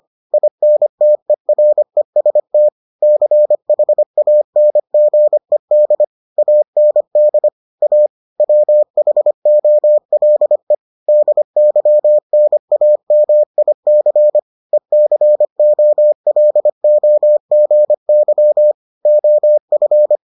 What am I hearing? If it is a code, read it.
Morse code, heard as a keyed tone, about 25 wpm, 600 Hz. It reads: INTEREST CHANGED AND A WHOLE DYNAMIC ECOLOGY OF